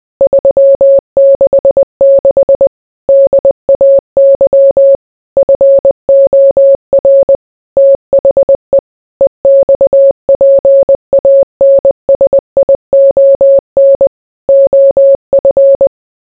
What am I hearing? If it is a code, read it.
Morse code, heard as a keyed tone, about 20 wpm, 550 Hz. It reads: 366 DAY FOR THE EXPANSION OF